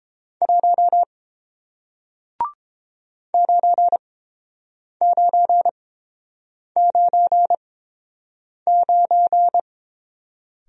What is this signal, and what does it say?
Morse code, heard as a keyed tone, about 30 wpm, spaced as for 7 wpm, 700 Hz. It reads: 1 9999